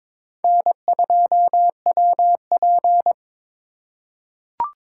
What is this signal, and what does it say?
Morse code, heard as a keyed tone, about 22 wpm, 700 Hz. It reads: N2WP
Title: USA Call Signs - 2x Character Spacing - 22wpm